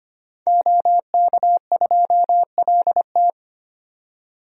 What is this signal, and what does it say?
Morse code, heard as a keyed tone, about 25 wpm, 700 Hz. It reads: OK2LT
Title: DXCC-100 Call Signs - 5 Characters - 25wpm